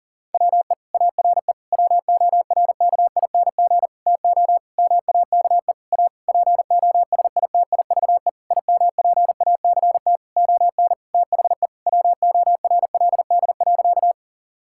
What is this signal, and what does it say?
Morse code, heard as a keyed tone, about 37 wpm, 700 Hz. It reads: WE ARE WORKING TO MAKE A POSITIVE IMPACT ON THE WORLD.
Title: Sentences from Top 1000 Words - Review Part 6 - 40wpm